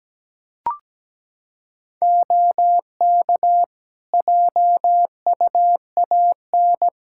Morse code, heard as a keyed tone, about 17 wpm, 700 Hz. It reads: OK JUAN
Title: Practice QSO - Exchange 4 - Ending - 17wpm